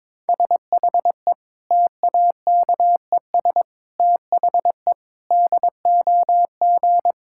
Morse code, heard as a keyed tone, about 22 wpm, 700 Hz. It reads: SHE TAKES THE DOG
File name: Difficult Sentences with Hints - 22wpm